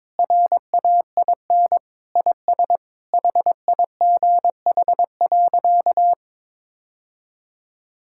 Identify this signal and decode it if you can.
Morse code, heard as a keyed tone, about 21 wpm, 700 Hz. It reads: RAIN IS HIGH.
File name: practice / 22wpm / Sentences with 3-Word Phrases - 22wpm